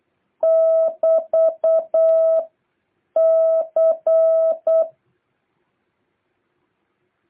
Morse code, heard as a keyed tone, about 8 wpm, 650 Hz. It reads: <BT> C
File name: jh8cbh / 2023cw